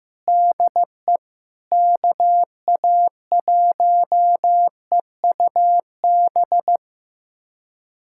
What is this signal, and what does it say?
Morse code, heard as a keyed tone, about 15 wpm, 700 Hz. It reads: DE KA1EUB
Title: USA Callsign DE Callsign 15wpm